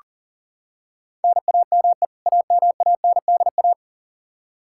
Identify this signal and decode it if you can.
Morse code, heard as a keyed tone, about 37 wpm, 700 Hz. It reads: NAME AMANDA
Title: Practice QSO - Exchange 1 - Beginning - 40wpm